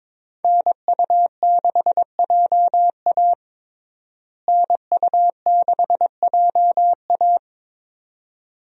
Morse code, heard as a keyed tone, about 22 wpm, 700 Hz. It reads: NU6JA NU6JA
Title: USA Call Signs - 5 Characters - Repeated 2x - 22wpm